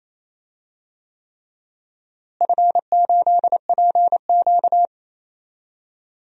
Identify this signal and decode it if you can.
Morse code, heard as a keyed tone, about 27 wpm, 700 Hz. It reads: F8PQ
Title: DXCC-100 Call Signs - 4 Characters - Repeated 3x - 28wpm